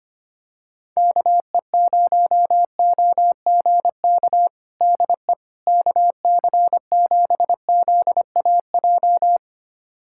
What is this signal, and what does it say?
Morse code, heard as a keyed tone, about 25 wpm, 700 Hz. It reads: KE0OGK DE KC7ZAJ
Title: USA Callsign DE Callsign - 6 Characters - 25wpm